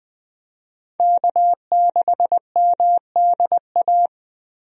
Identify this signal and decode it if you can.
Morse code, heard as a keyed tone, about 20 wpm, 700 Hz. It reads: K6MDA
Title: USA Call Signs - 5 Characters - Repeated 3x - 20wpm